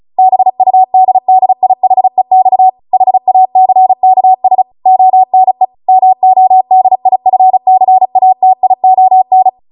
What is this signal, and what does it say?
Morse code, heard as a keyed tone, about 36 wpm, 750 Hz. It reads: BUDDIHEX HACKS ONE MODIFCATION